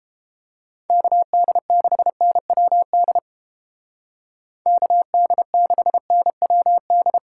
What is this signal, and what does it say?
Morse code, heard as a keyed tone, about 31 wpm, 700 Hz. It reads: KD6NWD KD6NWD